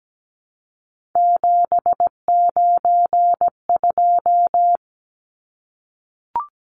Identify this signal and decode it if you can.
Morse code, heard as a keyed tone, about 17 wpm, 700 Hz. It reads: 792
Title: Numbers - 3-Digit 17wpm